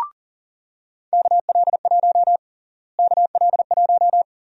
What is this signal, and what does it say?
Morse code, heard as a keyed tone, about 40 wpm, 700 Hz. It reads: KL1 KL1
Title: USA Call Sign Prefixes - Repeated 2x - 40wpm